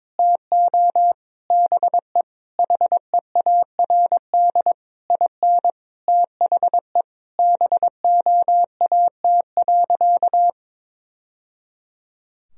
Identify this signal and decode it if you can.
Morse code, heard as a keyed tone, about 21 wpm, 700 Hz. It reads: TO BE HEARD IN THE BOAT.
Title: Sentences from Top 1000 Words - Review Part 3 - 22wpm